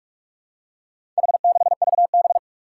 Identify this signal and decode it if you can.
Morse code, heard as a keyed tone, about 46 wpm, 700 Hz. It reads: H6VB
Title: Four Challenging Letter-Number Combinations 45wpm